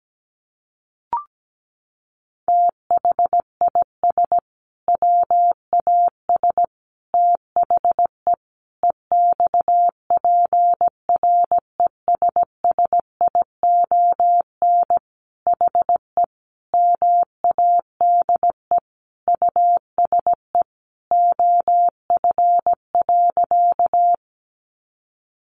Morse code, heard as a keyed tone, about 17 wpm, 700 Hz. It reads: THIS WAS THE EXPRESSION HE MADE USE OF.
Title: Sentences from Top 1200 Words 17wpm